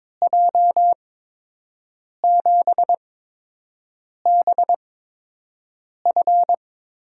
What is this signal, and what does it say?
Morse code, heard as a keyed tone, about 21 wpm, spaced as for 6 wpm, 700 Hz. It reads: J7BF
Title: USA Call Signs - 6 Characters - 8x Character Spacing - 22wpm